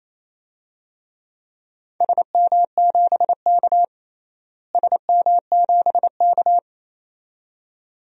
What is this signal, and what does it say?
Morse code, heard as a keyed tone, about 28 wpm, 700 Hz. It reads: SM7K SM7K
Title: DXCC-100 Call Signs - 4 Characters - Repeated 2x - 28wpm